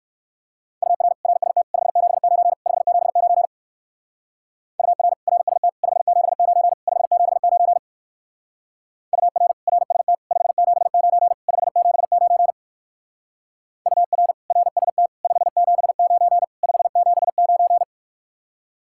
Morse code, heard as a keyed tone, about 54 wpm, 700 Hz. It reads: UR RST 579 579 UR RST 579 579 UR RST 579 579 UR RST 579 579
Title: Speed-Racing Common QSO Phrases - 45wpm